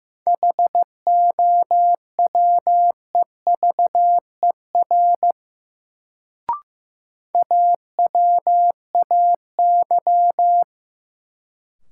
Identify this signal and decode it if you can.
Morse code, heard as a keyed tone, about 15 wpm, 700 Hz. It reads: HOWEVER AWAY
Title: Sentences from Top 600 Words - New Word Drill - 15wpm